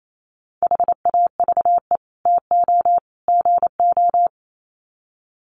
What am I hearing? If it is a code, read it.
Morse code, heard as a keyed tone, about 28 wpm, 700 Hz. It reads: HAVE TO GO
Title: Sets of 3 Words 28wpm